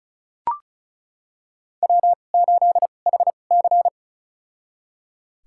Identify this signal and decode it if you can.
Morse code, heard as a keyed tone, about 36 wpm, spaced as for 25 wpm, 700 Hz. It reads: W8HC
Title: USA Call Signs - 2x Character Spacing - 35wpm